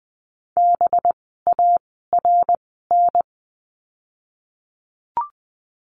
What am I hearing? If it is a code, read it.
Morse code, heard as a keyed tone, about 20 wpm, spaced as for 15 wpm, 700 Hz. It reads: BARN
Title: 185 Lesson 30f - Letter b - Words 20wpm